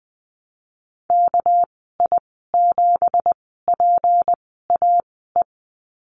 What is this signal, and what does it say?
Morse code, heard as a keyed tone, about 20 wpm, spaced as for 14 wpm, 700 Hz. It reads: KI7PAE
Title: 231 Lesson 37g - Letter x - Call Signs 20wpm